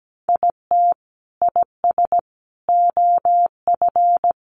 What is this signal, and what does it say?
Morse code, heard as a keyed tone, about 17 wpm, 700 Hz. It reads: IT IS OF
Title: Sentences from Top 1500 Words 17wpm